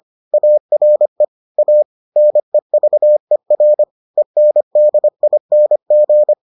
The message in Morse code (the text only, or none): ARE A NEVER ENDING